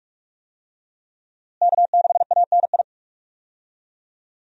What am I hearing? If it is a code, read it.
Morse code, heard as a keyed tone, about 42 wpm, 700 Hz. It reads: K6ANI
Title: USA Call Signs - 5 Characters - Repeated 3x - 45wpm